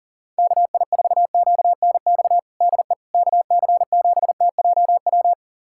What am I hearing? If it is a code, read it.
Morse code, heard as a keyed tone, about 37 wpm, 700 Hz. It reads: KI4QNX DE KC7TJW
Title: USA Callsign DE Callsign - 6 Characters - 40wpm